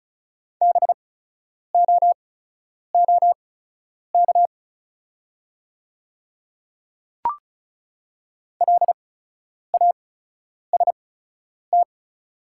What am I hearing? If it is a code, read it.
Morse code, heard as a keyed tone, about 36 wpm, spaced as for 10 wpm, 700 Hz. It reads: BOOK LAST
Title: 8x Character Spacing - Top 500 Words - 35wpm